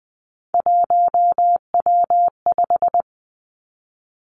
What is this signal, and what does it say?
Morse code, heard as a keyed tone, about 20 wpm, 700 Hz. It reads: 1W5
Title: Three Challenging Letter-Number Combinations 20wpm